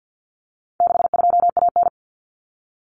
Morse code, heard as a keyed tone, about 50 wpm, 700 Hz. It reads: 62AN
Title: Common Cut Numbers - 4-Digit 50wpm